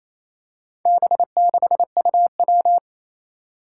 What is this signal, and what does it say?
Morse code, heard as a keyed tone, about 28 wpm, 700 Hz. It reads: B6UW
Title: Four Challenging Letter-Number Combinations 28wpm